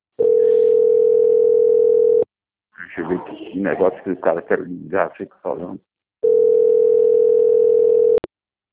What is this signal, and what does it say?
Ringback tone (North America), 440 + 480 Hz, with speech.
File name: out- - - - - .7867